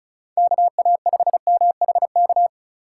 Morse code, heard as a keyed tone, about 36 wpm, 700 Hz. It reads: KA5MHK
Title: USA Call Signs - 6 Characters - 2x Character Spacing - 35wpm